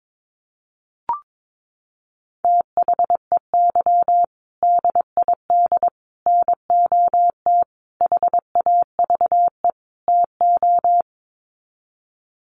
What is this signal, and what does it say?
Morse code, heard as a keyed tone, about 22 wpm, 700 Hz. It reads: THEY DID NOT HAVE TO
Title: Sets of 5 Words Encore2 22wpm